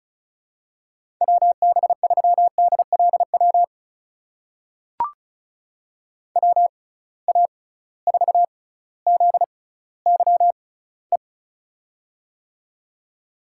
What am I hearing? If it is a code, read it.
Morse code, heard as a keyed tone, about 36 wpm, spaced as for 18 wpm, 700 Hz. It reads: WB3DLW WA4ZYE
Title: USA Call Signs - 6 Characters - 6x Character Spacing - 35wpm